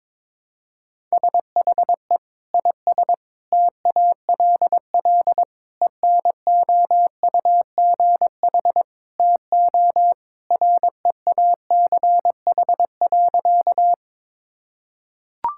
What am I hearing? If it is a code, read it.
Morse code, heard as a keyed tone, about 22 wpm, 700 Hz. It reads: SHE IS TALL ENOUGH TO REACH.
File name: practice / 22wpm / Sentences with 2-Word Phrases - 22wpm